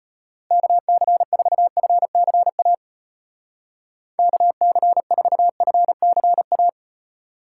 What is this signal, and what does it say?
Morse code, heard as a keyed tone, about 33 wpm, 700 Hz. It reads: KC4FCA KC4FCA